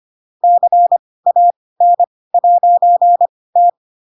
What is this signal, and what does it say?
Morse code, heard as a keyed tone, about 25 wpm, spaced as for 18 wpm, 700 Hz. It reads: CAN'T
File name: Pinocchio / Pinocchio ch7 12 25 ik1hge